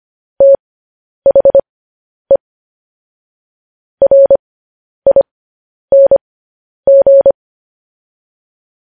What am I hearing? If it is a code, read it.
Morse code, heard as a keyed tone, about 25 wpm, spaced as for 10 wpm, 550 Hz. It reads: THE RING
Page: news-headlines-in-morse-code-at-10-wpm